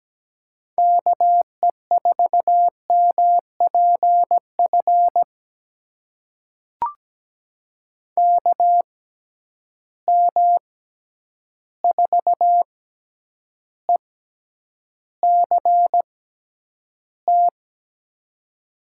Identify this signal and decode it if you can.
Morse code, heard as a keyed tone, about 17 wpm, spaced as for 9 wpm, 700 Hz. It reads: KE4MPF KM4ECT
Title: USA Call Signs - 6x Character Spacing - 17wpm